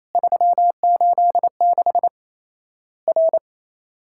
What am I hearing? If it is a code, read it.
Morse code, heard as a keyed tone, about 28 wpm, 700 Hz. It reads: 386 R
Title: Worked All Europe DX Contest - QTC Exchange - 28wpm